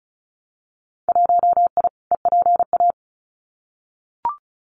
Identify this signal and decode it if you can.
Morse code, heard as a keyed tone, about 36 wpm, 700 Hz. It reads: 1I EPA
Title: Winter Field Day Exchanges Encore 35wpm